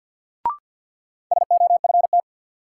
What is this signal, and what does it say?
Morse code, heard as a keyed tone, about 52 wpm, 700 Hz. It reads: SOFT
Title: Sentences from Top 1500 Words - New Word Drill - 50wpm